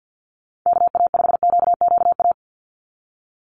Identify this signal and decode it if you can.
Morse code, heard as a keyed tone, about 50 wpm, 700 Hz. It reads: KA5QQA
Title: NA SOTA Chaser Callsigns 50wpm